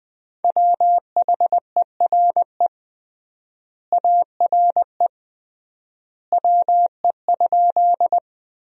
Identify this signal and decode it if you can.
Morse code, heard as a keyed tone, about 20 wpm, 700 Hz. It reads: WHERE ARE WE?
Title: 3x Word-Spacing - Sentences from Top 200 Words - 20wpm